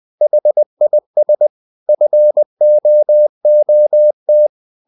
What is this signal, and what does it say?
Morse code, heard as a keyed tone, about 20 wpm, 600 Hz. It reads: HIS FOOT
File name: the old man and the sea / the old man and the sea part 1 20wpm